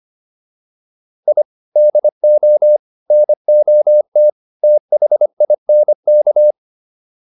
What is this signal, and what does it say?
Morse code, heard as a keyed tone, about 25 wpm, 600 Hz. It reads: I DO NOT THINK